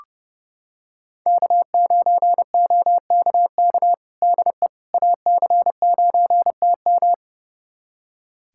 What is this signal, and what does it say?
Morse code, heard as a keyed tone, about 30 wpm, 700 Hz. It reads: K9OKK DE AC9TM
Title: USA Callsign DE Callsign - 5 Characters - 30wpm